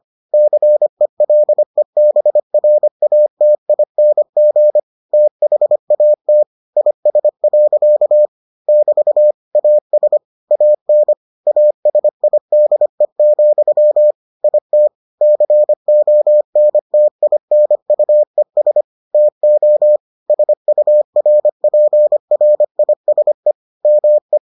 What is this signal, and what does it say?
Morse code, heard as a keyed tone, about 25 wpm, 600 Hz. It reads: CELEBRATING THAT IS. <BT> AS AN ASIDE, IT CONTINUES TO SURPRISE ME